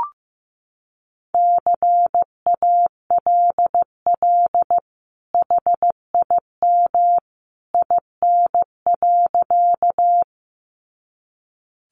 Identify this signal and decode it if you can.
Morse code, heard as a keyed tone, about 15 wpm, 700 Hz. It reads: CALL HIM IN.
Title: Sentences from Top 100 Words 15wpm